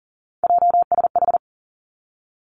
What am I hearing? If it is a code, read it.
Morse code, heard as a keyed tone, about 41 wpm, 700 Hz. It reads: JSH
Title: Three Challenging Letter-Number Combinations 40wpm